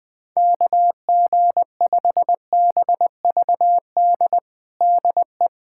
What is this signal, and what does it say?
Morse code, heard as a keyed tone, about 20 wpm, 700 Hz. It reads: KG5BVD DE
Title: USA Callsign DE Callsign 20wpm